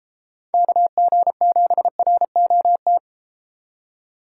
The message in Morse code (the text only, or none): KG7ROT